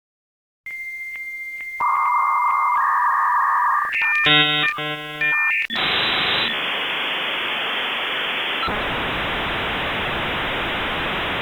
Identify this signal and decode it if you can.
Dial-up modem handshake.